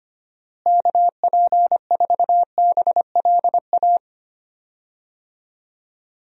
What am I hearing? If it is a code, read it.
Morse code, heard as a keyed tone, about 25 wpm, 700 Hz. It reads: KP4BLA
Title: DXCC-100 Call Signs - 6 Characters - Repeated 3x - 25wpm